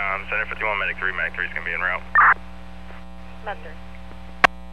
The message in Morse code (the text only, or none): none